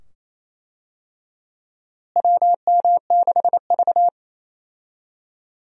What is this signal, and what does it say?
Morse code, heard as a keyed tone, about 28 wpm, 700 Hz. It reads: WM6V
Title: USA Call Signs - 4 Characters - Repeated 2x - 28wpm